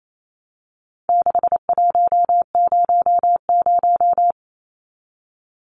Morse code, heard as a keyed tone, about 28 wpm, 700 Hz. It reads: 6100
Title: Numbers - 4-Digit 28wpm